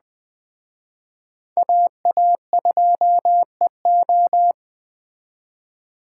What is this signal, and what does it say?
Morse code, heard as a keyed tone, about 21 wpm, 700 Hz. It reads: AA2EO
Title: Speed Racing - USA Call Signs 17wpm